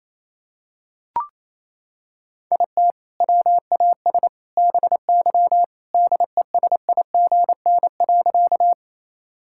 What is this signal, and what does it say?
Morse code, heard as a keyed tone, about 27 wpm, 700 Hz. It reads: IT WAS BY DESIGN.
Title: Sentences from Top 1000 Words - Review Part 5 - 28wpm